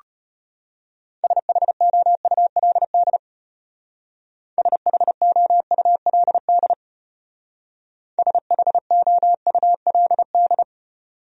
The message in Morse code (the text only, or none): SHOULD SHOULD SHOULD